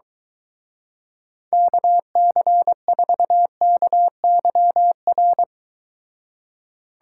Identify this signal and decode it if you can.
Morse code, heard as a keyed tone, about 23 wpm, 700 Hz. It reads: KC4KYR